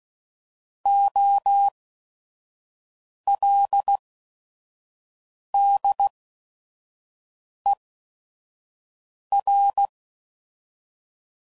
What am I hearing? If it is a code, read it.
Morse code, heard as a keyed tone, about 16 wpm, spaced as for 5 wpm, 800 Hz. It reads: OLDER
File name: Prefixes / By First Letter / Os